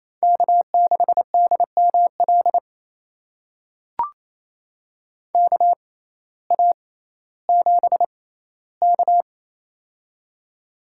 Morse code, heard as a keyed tone, about 28 wpm, spaced as for 15 wpm, 700 Hz. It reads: K6DML KA7K
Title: USA Call Signs - 6x Character Spacing - 28wpm